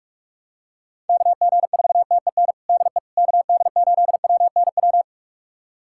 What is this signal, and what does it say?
Morse code, heard as a keyed tone, about 41 wpm, 700 Hz. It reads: KG4TEN DE KD8WNW
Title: USA Callsign DE Callsign - 6 Characters - 45wpm